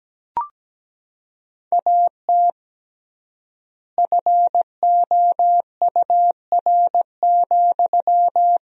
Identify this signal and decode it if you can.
Morse code, heard as a keyed tone, about 17 wpm, 700 Hz. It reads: AT FOUR,
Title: 3x Word-Spacing - Sentences from Top 300 Words - 17wpm